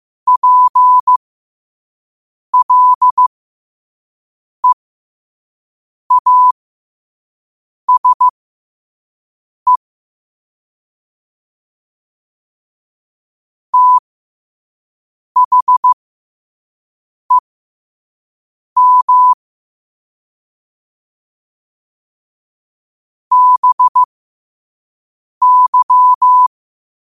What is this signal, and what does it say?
Morse code, heard as a keyed tone, about 15 wpm, spaced as for 5 wpm, 1000 Hz. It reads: PLEASE THEM BY
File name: qotdfiles / 5wpm